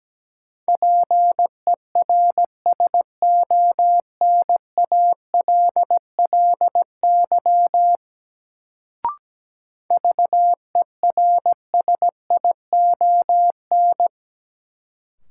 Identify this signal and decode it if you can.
Morse code, heard as a keyed tone, about 17 wpm, 700 Hz. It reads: PERSONALLY VERSION
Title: Sentences from Top 1400 Words - New Word Drill Alt - 17wpm